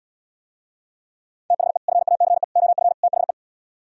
Morse code, heard as a keyed tone, about 77 wpm, 700 Hz. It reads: THE FACE OF THE